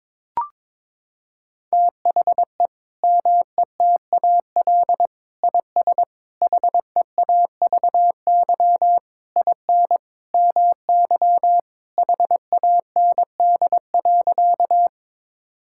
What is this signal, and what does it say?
Morse code, heard as a keyed tone, about 22 wpm, 700 Hz. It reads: THE METAL IS HEAVY IN MY HAND.
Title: Sentences from Top 1000 Words - Review Part 6 - 22wpm